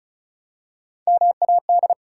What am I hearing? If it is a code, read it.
Morse code, heard as a keyed tone, about 36 wpm, 700 Hz. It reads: MAD
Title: Three Letter Words - 35wpm